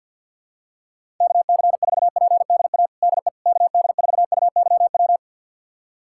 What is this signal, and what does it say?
Morse code, heard as a keyed tone, about 47 wpm, 700 Hz. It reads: KC4PDA DE KD4UYW
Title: USA Callsign DE Callsign - 6 Characters - 50wpm